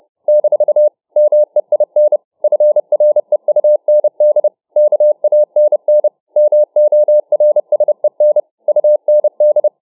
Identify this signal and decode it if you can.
Morse code, heard as a keyed tone, about 30 wpm, 600 Hz. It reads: - MEIN FREUND KANN MORSEN UND